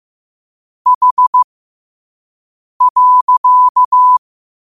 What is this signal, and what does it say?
Morse code, heard as a keyed tone, about 15 wpm, spaced as for 5 wpm, 1000 Hz. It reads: H.
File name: qotdfiles / 5wpm